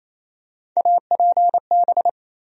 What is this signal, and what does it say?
Morse code, heard as a keyed tone, about 28 wpm, 700 Hz. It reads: APB
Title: USA Call Sign Postfixes - 3 Letters 28wpm